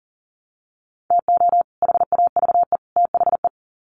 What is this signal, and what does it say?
Morse code, heard as a keyed tone, about 40 wpm, 700 Hz. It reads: TO HAVE THE
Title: Sets of 3 Words 40wpm